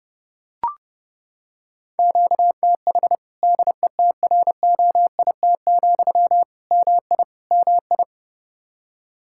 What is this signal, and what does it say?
Morse code, heard as a keyed tone, about 32 wpm, 700 Hz. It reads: QTH DETROIT, MI MI
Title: Practice QSO - Exchange 1 - Beginning - 30wpm